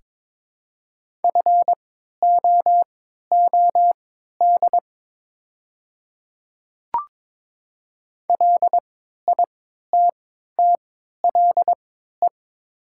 Morse code, heard as a keyed tone, about 22 wpm, spaced as for 12 wpm, 700 Hz. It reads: FOOD LITTLE
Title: 3x Character Spacing - Top 300 Words - 22wpm